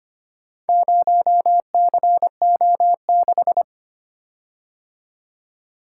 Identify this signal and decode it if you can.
Morse code, heard as a keyed tone, about 25 wpm, 700 Hz. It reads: 0CO6
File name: practice / 25wpm / Four Letter-Number Combinations - 25wpm